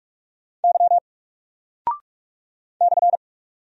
Morse code, heard as a keyed tone, about 44 wpm, 700 Hz. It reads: Y /
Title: Single Character (Letter:Number:Punctuation:Prosign) 45wpm